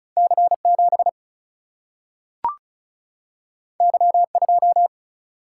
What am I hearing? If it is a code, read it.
Morse code, heard as a keyed tone, about 36 wpm, 700 Hz. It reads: C 7 Y 2
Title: Single-Letter Single-Number 35wpm